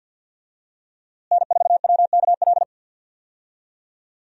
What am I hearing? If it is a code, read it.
Morse code, heard as a keyed tone, about 46 wpm, 700 Hz. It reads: N4WKL
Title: USA Call Signs - 5 Characters - Repeated 3x - 50wpm